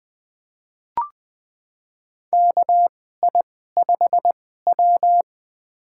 Morse code, heard as a keyed tone, about 20 wpm, spaced as for 14 wpm, 700 Hz. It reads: KI5W
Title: USA Call Signs - 4 Characters - 2x Character Spacing - 20wpm